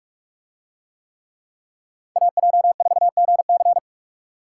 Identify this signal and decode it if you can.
Morse code, heard as a keyed tone, about 46 wpm, 700 Hz. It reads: AJ4GC